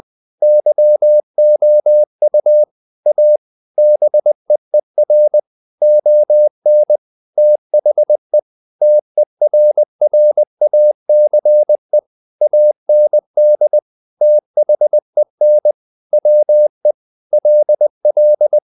Morse code, heard as a keyed tone, about 20 wpm, 600 Hz. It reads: YOU A BEER ON THE TERRACE AND THEN WE LL
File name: the old man and the sea / the old man and the sea part 1 20wpm